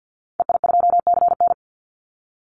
Morse code, heard as a keyed tone, about 49 wpm, 700 Hz. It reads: EI2CN
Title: Call Signs - 50wpm